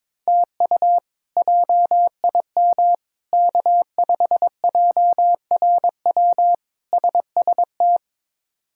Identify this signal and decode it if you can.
Morse code, heard as a keyed tone, about 22 wpm, 700 Hz. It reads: TU JIM K5JRW SST
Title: K1USN SST QSO Practice - 22wpm